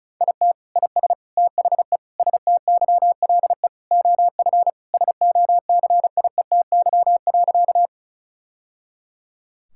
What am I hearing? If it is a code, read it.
Morse code, heard as a keyed tone, about 36 wpm, 700 Hz. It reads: IT IS THE STYLE OF SOCIETY.
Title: Sentences from Top 1000 Words - Review Part 5 - 35wpm